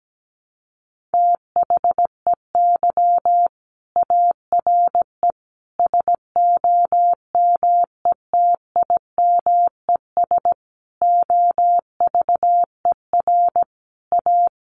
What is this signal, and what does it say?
Morse code, heard as a keyed tone, about 17 wpm, 700 Hz. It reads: THEY ARE SOMETIMES OVER A
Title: Sentences from Top 1300 Words 17wpm